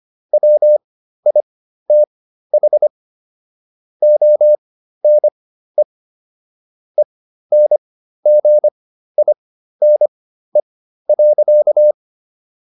Morse code, heard as a keyed tone, about 25 wpm, spaced as for 13 wpm, 600 Hz. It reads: WITH ONE ENGINE.